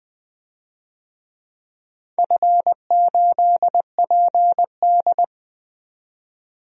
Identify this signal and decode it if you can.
Morse code, heard as a keyed tone, about 20 wpm, 700 Hz. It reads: F8PD
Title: DXCC-100 Call Signs - 4 Characters - 20wpm